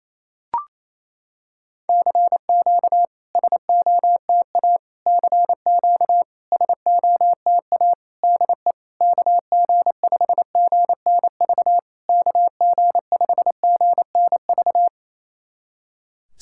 Morse code, heard as a keyed tone, about 28 wpm, 700 Hz. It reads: CQ SOTA CQ SOTA DE KG5GNV KG5GNV
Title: SOTA QSO Practice 28wpm